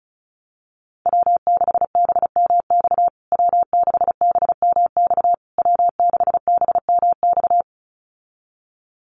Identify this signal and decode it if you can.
Morse code, heard as a keyed tone, about 34 wpm, 700 Hz. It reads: W6BMX W6BMX W6BMX
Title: USA Call Signs Repeated 3x 35wpm